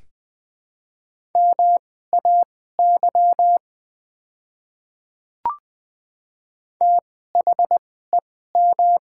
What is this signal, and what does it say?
Morse code, heard as a keyed tone, about 20 wpm, spaced as for 14 wpm, 700 Hz. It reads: MAY THEM
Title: 2x Character Spacing - Top 100 Words - 20wpm